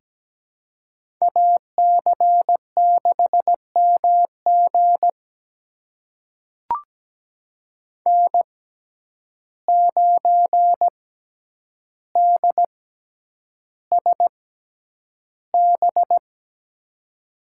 Morse code, heard as a keyed tone, about 17 wpm, spaced as for 9 wpm, 700 Hz. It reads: AC6MG N9DSB
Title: USA Call Signs - 5 Characters - 6x Character Spacing - 17wpm